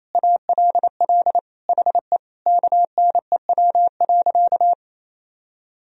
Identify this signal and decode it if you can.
Morse code, heard as a keyed tone, about 28 wpm, 700 Hz. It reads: ALL HE KNEW.
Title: Sentences from Top 400 Words and Abbreviations 28wpm